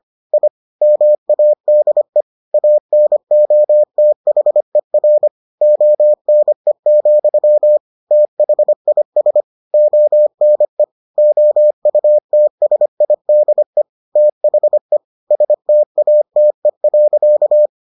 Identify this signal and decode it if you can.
Morse code, heard as a keyed tone, about 25 wpm, 600 Hz. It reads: I MADE ANOTHER ONE, THIS ONE OUTSIDE THE STATE.